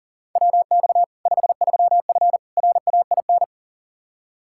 Morse code, heard as a keyed tone, about 40 wpm, 700 Hz. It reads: WX 53F RAIN
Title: Practice QSO - Exchange 2 - Info - 40wpm